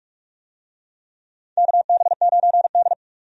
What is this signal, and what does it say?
Morse code, heard as a keyed tone, about 46 wpm, 700 Hz. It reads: KB9D
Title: USA Call Signs - 4 Characters - 45wpm